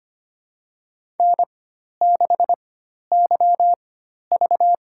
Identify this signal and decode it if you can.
Morse code, heard as a keyed tone, about 25 wpm, spaced as for 12 wpm, 700 Hz. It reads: N6YV
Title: USA Call Signs - 4 Characters - 4x Character Spacing - 25wpm